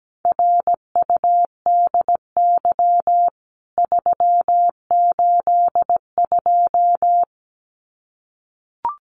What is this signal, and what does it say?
Morse code, heard as a keyed tone, about 17 wpm, 700 Hz. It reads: RUDY 382
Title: CWT Exchange 17wpm